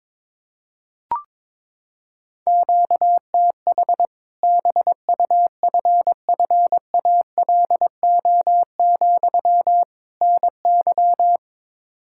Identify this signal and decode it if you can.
Morse code, heard as a keyed tone, about 22 wpm, 700 Hz. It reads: QTH BUFFALO, NY
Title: Practice QSO - Exchange 1 - Beginning - 22wpm